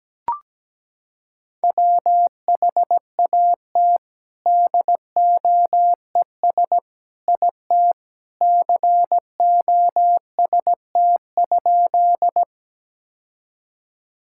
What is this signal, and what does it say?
Morse code, heard as a keyed tone, about 17 wpm, 700 Hz. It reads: WHAT DOES IT COST?
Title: Sentences from Top 1000 Words - Review Part 5 - 17wpm